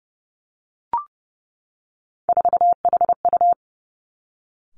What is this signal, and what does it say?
Morse code, heard as a keyed tone, about 30 wpm, 700 Hz. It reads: 4HU
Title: Three Challenging Letter-Number Combinations 30wpm